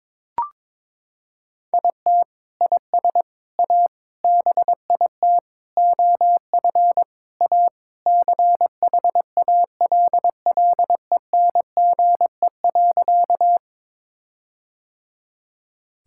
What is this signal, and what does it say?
Morse code, heard as a keyed tone, about 22 wpm, 700 Hz. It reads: IT IS A BIT OF A CHALLENGE.
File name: practice / 22wpm / Sentences with 4-Word Phrases - 22wpm